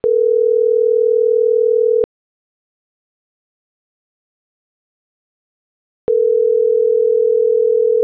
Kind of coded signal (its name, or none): telephone call progress tone